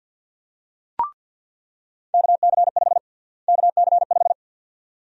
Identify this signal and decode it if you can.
Morse code, heard as a keyed tone, about 47 wpm, 700 Hz. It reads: KC5 KC5
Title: USA Call Sign Prefixes - Repeated 2x - 50wpm